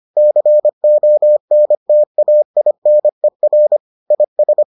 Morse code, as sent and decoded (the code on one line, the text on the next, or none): -.-. --- -. - .- .. -. .-. / .. ...
CONTAINER IS